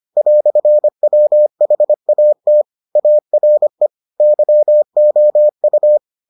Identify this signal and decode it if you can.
Morse code, heard as a keyed tone, about 25 wpm, 600 Hz. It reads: "WHAT ARE YOU